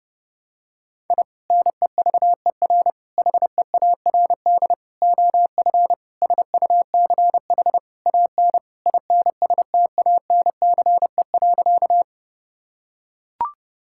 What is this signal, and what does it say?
Morse code, heard as a keyed tone, about 31 wpm, 700 Hz. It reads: I NEVER HEARD OF SUCH AN INSTANCE.
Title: Sentences from Top 1000 Words - Review Part 5 - 30wpm